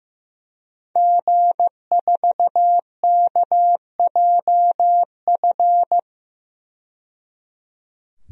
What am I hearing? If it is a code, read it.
Morse code, heard as a keyed tone, about 16 wpm, 700 Hz. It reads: G4KJF
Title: DXCC-100 Call Signs - 5 Characters - 15wpm